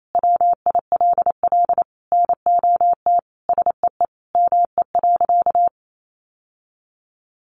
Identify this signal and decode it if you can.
Morse code, heard as a keyed tone, about 28 wpm, 700 Hz. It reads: WILL NOT SEE ME.
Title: Sentences from Top 600 Words 28wpm